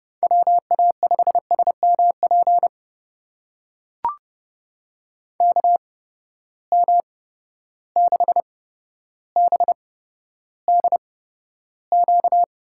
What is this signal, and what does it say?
Morse code, heard as a keyed tone, about 31 wpm, spaced as for 13 wpm, 700 Hz. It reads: WA5SMP KM6BDQ
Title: USA Call Signs - 6 Characters - 8x Character Spacing - 30wpm